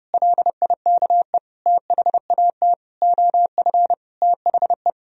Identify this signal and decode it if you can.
Morse code, heard as a keyed tone, about 30 wpm, 700 Hz. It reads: LIKE THAT OF THE
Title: Sentences from Top 1000 Words - Review Part 2 - 30wpm